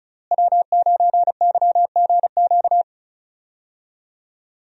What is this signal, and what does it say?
Morse code, heard as a keyed tone, about 36 wpm, 700 Hz. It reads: W9YGQ
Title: USA Call Signs - 5 Characters - Repeated 3x - 35wpm